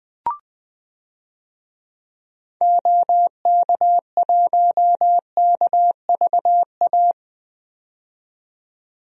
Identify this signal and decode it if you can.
Morse code, heard as a keyed tone, about 20 wpm, 700 Hz. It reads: OK1KVA
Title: DXCC-100 Call Signs - 6 Characters - 20wpm